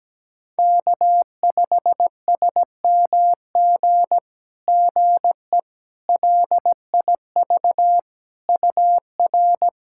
Morse code, heard as a keyed tone, about 17 wpm, 700 Hz. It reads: K5SMG GE LIV UR
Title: SOTA QSO Practice 17wpm